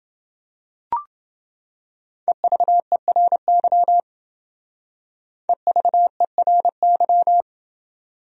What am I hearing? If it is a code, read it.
Morse code, heard as a keyed tone, about 28 wpm, 700 Hz. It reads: EVERY EVERY